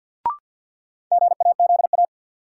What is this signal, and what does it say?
Morse code, heard as a keyed tone, about 46 wpm, 700 Hz. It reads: GAZA
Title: Sets of 5 Words - New Word Drill - 50wpm